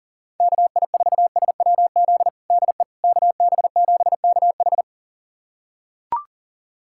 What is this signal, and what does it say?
Morse code, heard as a keyed tone, about 40 wpm, 700 Hz. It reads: KI4SWZ DE KB7KH
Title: USA Callsign DE Callsign 40wpm